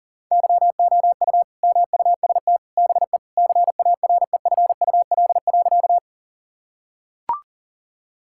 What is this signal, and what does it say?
Morse code, heard as a keyed tone, about 40 wpm, 700 Hz. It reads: YOU MUST BE CAREFUL.
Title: Sentences from Top 1600 Words 40wpm